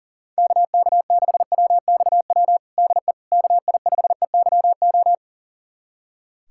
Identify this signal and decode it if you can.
Morse code, heard as a keyed tone, about 40 wpm, 700 Hz. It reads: KK6WXW DE KI5EYO
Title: USA Callsign DE Callsign - 6 Characters - 40wpm